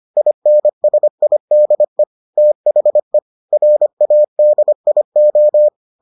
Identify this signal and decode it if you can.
Morse code, heard as a keyed tone, about 25 wpm, 600 Hz. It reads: INSIDE THE RADIO